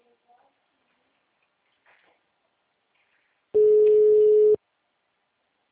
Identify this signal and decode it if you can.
Ringback tone (Europe, 425 Hz), with speech.